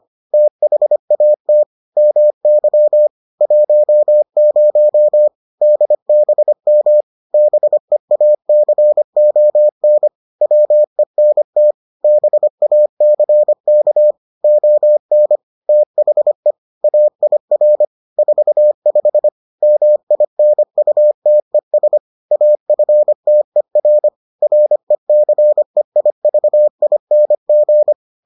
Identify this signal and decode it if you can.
Morse code, heard as a keyed tone, about 25 wpm, 600 Hz. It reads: THAT MY 10 DBM BEACON WENT BACK ON THE AIR 45 MINUTES AFTER RECEIVING